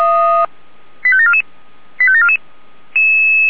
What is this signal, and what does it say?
Five-tone selective call (ZVEI), heard as quick runs of five tones, tones of about 70 ms. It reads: 85744 85744